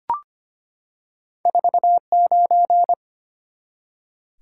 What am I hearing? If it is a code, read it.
Morse code, heard as a keyed tone, about 25 wpm, 700 Hz. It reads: 49
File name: Two Letter-Number Combinations - 25wpm